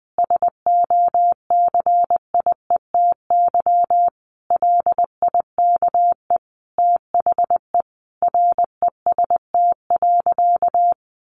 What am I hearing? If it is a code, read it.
Morse code, heard as a keyed tone, about 20 wpm, 700 Hz. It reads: SOCIETY LIKE THE REST.
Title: Sentences from Top 1100 Words 20wpm